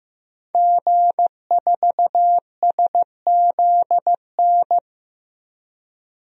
Morse code, heard as a keyed tone, about 15 wpm, 700 Hz. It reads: G4SZN
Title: DXCC-100 Call Signs - 5 Characters - 15wpm